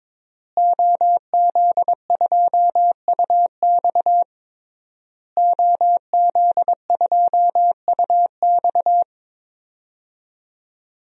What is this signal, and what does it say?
Morse code, heard as a keyed tone, about 22 wpm, 700 Hz. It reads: OZ2UX OZ2UX
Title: DXCC-100 Call Signs - 5 Characters - Repeated 3x - 22wpm